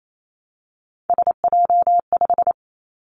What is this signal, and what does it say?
Morse code, heard as a keyed tone, about 28 wpm, 700 Hz. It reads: SJ5
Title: Three Challenging Letter-Number Combinations 28wpm